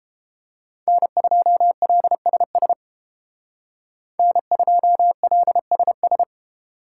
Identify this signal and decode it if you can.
Morse code, heard as a keyed tone, about 31 wpm, 700 Hz. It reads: N2LSS N2LSS